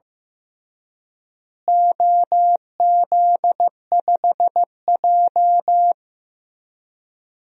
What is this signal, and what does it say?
Morse code, heard as a keyed tone, about 15 wpm, 700 Hz. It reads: OZ5J
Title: DXCC-100 Call Signs - 4 Characters - Repeated 2x - 15wpm